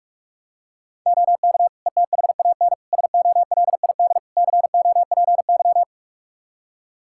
Morse code, heard as a keyed tone, about 42 wpm, 700 Hz. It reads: OK ETHAN SOLID COPY
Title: Practice QSO - Exchange 3 - More Info - 45wpm